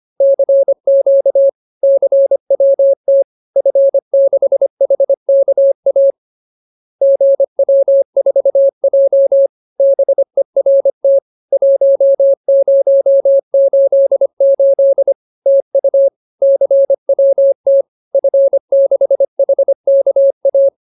Morse code, heard as a keyed tone, about 25 wpm, 550 Hz. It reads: CQ CWT F6HKA GW4J BERT 1088 TU CWT F6HKA